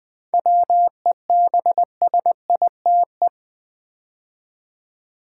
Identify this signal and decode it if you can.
Morse code, heard as a keyed tone, about 20 wpm, 700 Hz. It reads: WEBSITE
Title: 7 Letter Words - First 3 Letters Given - 20wpm